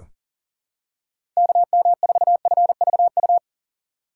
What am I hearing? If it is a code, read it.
Morse code, heard as a keyed tone, about 40 wpm, 700 Hz. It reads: KM4FVU
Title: USA Call Signs - 6 Characters - 4x Character Spacing - 40wpm